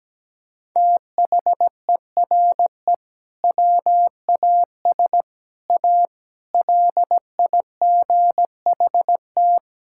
Morse code, heard as a keyed tone, about 17 wpm, 700 Hz. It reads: THERE WAS A LIGHT
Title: Sentences from Top 1000 Words - Review Part 6 - 17wpm